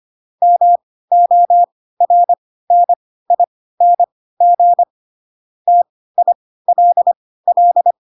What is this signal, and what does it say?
Morse code, heard as a keyed tone, about 25 wpm, spaced as for 16 wpm, 700 Hz. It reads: MORNING TILL
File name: Pinocchio / Pinocchio ch4 10 25 ik1hge